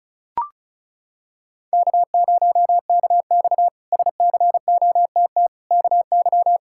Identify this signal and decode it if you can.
Morse code, heard as a keyed tone, about 36 wpm, 700 Hz. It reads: K0KX SCOTT KY
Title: K1USN SST QSO Practice - 35wpm